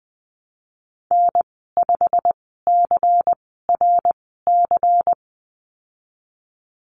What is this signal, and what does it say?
Morse code, heard as a keyed tone, about 20 wpm, spaced as for 15 wpm, 700 Hz. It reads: N5CRC
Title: 179 Lesson 29g - Slash - Call Signs 20wpm